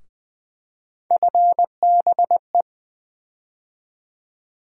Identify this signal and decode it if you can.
Morse code, heard as a keyed tone, about 20 wpm, 700 Hz. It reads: FBE
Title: USA Call Sign Postfixes - 3 Letters 20wpm